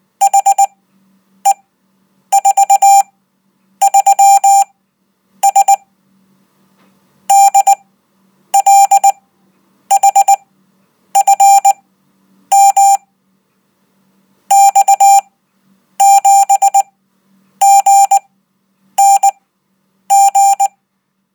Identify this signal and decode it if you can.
Morse code, heard as a keyed tone, about 19 wpm, spaced as for 8 wpm, 790 Hz. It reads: HE43S DLHFM X7GNG